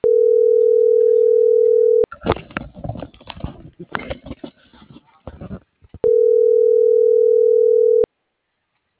Ringback tone (North America), 440 + 480 Hz.